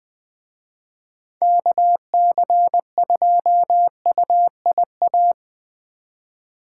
Morse code, heard as a keyed tone, about 23 wpm, 700 Hz. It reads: KC2UIA
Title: Speed Racing - USA 2x3 Call Signs 20wpm